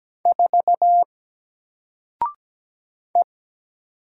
Morse code, heard as a keyed tone, about 17 wpm, 700 Hz. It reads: 4 E
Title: Single Letter-Number 17wpm